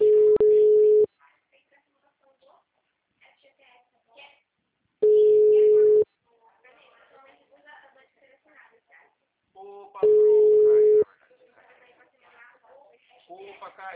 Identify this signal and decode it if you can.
Ringback tone (Europe, 425 Hz), with speech.
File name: out- -2003- - - .68926